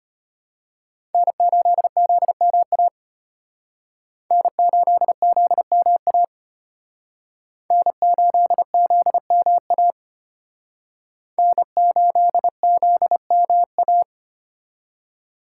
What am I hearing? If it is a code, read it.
Morse code, heard as a keyed tone, about 32 wpm, 700 Hz. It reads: N8ZMA N8ZMA N8ZMA N8ZMA